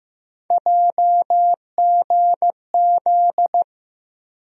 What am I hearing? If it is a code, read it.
Morse code, heard as a keyed tone, about 15 wpm, 700 Hz. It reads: JGZ
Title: USA Call Sign Postfixes - 3 Letters 15wpm